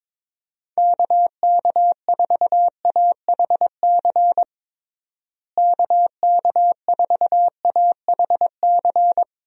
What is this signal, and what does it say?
Morse code, heard as a keyed tone, about 22 wpm, 700 Hz. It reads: KK4AHC KK4AHC
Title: USA Call Signs - 6 Characters - Repeated 3x - 22wpm